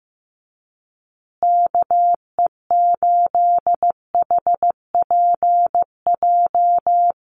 Morse code, heard as a keyed tone, about 15 wpm, 700 Hz. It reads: KE8HPJ
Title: USA Call Signs Repeated 3x 15wpm